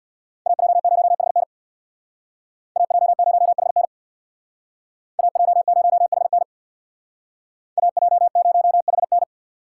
Morse code, heard as a keyed tone, about 64 wpm, 700 Hz. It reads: AJ0HN AJ0HN AJ0HN AJ0HN